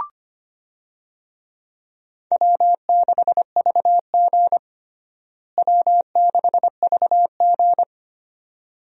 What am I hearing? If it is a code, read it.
Morse code, heard as a keyed tone, about 25 wpm, 700 Hz. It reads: W6VG W6VG
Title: USA Call Signs - 4 Characters - Repeated 2x - 25wpm